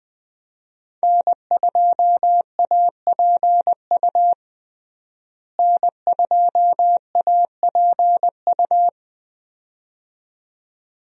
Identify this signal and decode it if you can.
Morse code, heard as a keyed tone, about 20 wpm, 700 Hz. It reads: N2APU N2APU
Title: USA Call Signs - 5 Characters - Repeated 2x - 20wpm